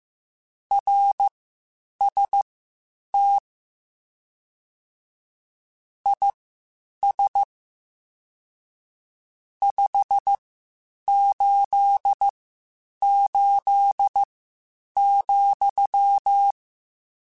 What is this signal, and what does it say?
Morse code, heard as a keyed tone, about 15 wpm, spaced as for 8 wpm, 800 Hz. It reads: RST IS 588,
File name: (spaced) / QSOs Spaced-mp3s / WA7QOX 1 @13 sp